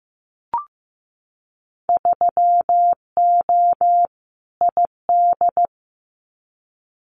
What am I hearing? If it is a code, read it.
Morse code, heard as a keyed tone, about 15 wpm, 700 Hz. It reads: 3O ID
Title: Winter Field Day Exchanges 15wpm